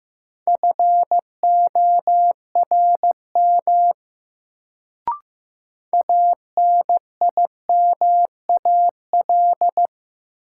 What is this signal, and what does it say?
Morse code, heard as a keyed tone, about 15 wpm, 700 Hz. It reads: FORM ANIMAL
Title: Sentences from Top 1000 Words - Review Part 1 - New Word Drill - 15wpm